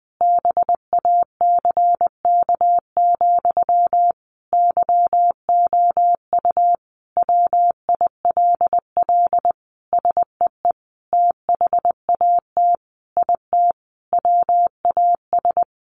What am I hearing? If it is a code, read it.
Morse code, heard as a keyed tone, about 20 wpm, 700 Hz. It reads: BACK, YOU WILL SEE THAT IT WAS
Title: Sentences from Top 800 Words 20wpm